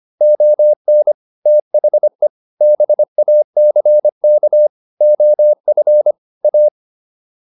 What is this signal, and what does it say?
Morse code, heard as a keyed tone, about 25 wpm, 600 Hz. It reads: ON THE BACK OF A